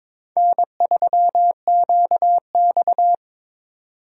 Morse code, heard as a keyed tone, about 26 wpm, 700 Hz. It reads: N3QX
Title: Speed Racing - USA Call Signs 22wpm